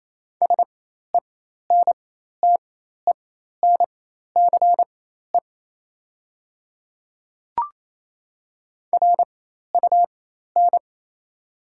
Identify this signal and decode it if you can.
Morse code, heard as a keyed tone, about 28 wpm, spaced as for 13 wpm, 700 Hz. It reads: SENTENCE RUN
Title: 4x Character Spacing - Top 300 Words - 28wpm